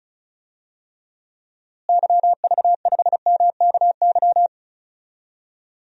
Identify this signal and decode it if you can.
Morse code, heard as a keyed tone, about 36 wpm, 700 Hz. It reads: YV5MKY
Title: DXCC-100 Call Signs - 6 Characters - Repeated 3x - 35wpm